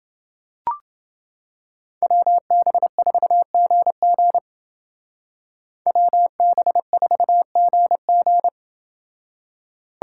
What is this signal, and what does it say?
Morse code, heard as a keyed tone, about 28 wpm, 700 Hz. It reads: WB4GG WB4GG